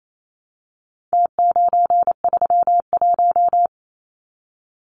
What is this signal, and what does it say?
Morse code, heard as a keyed tone, about 28 wpm, 700 Hz. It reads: T931
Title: Numbers - Contest Style - 4-Digit 28wpm